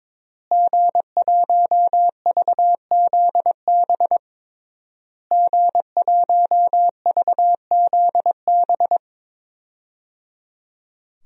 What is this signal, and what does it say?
Morse code, heard as a keyed tone, about 22 wpm, 700 Hz. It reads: G1VZB G1VZB
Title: DXCC-100 Call Signs - 5 Characters - Repeated 3x - 22wpm